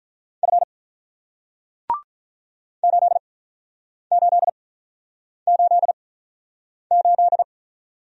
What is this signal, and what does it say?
Morse code, heard as a keyed tone, about 48 wpm, spaced as for 9 wpm, 700 Hz. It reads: F 8888